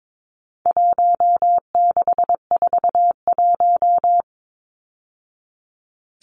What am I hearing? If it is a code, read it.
Morse code, heard as a keyed tone, about 22 wpm, 700 Hz. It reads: 1641
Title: Numbers - Contest Style - 4-Digit 22wpm